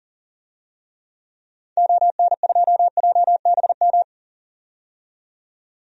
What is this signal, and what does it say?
Morse code, heard as a keyed tone, about 40 wpm, 700 Hz. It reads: ON2JBM
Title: DXCC-100 Call Signs - 6 Characters - 40wpm